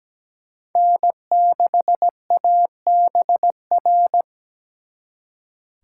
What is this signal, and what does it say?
Morse code, heard as a keyed tone, about 20 wpm, 700 Hz. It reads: N6ABR
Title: Speed Racing - USA 1x3 Call Signs 17wpm